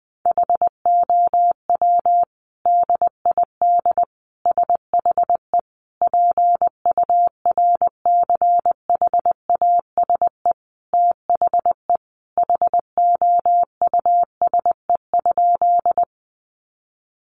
Morse code, heard as a keyed tone, about 20 wpm, 700 Hz. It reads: HOW DID SHE PURCHASE THE HOUSE?
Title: Sentences from Top 800 Words 20wpm